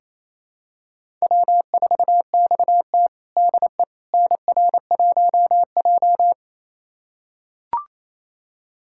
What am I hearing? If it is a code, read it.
Morse code, heard as a keyed tone, about 27 wpm, 700 Hz. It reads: W4XT DE NR1J
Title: USA Callsign DE Callsign - 4 Characters - 28wpm